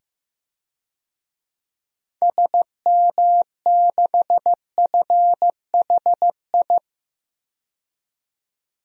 Morse code, heard as a keyed tone, about 15 wpm, 700 Hz. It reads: SM6FHI
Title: DXCC-100 Call Signs - 6 Characters - 15wpm